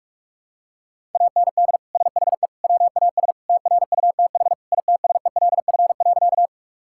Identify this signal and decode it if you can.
Morse code, heard as a keyed tone, about 46 wpm, 700 Hz. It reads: AND SHE WAS TRUTH ITSELF.
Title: Sentences from Top 1000 Words - Review Part 5 - 45wpm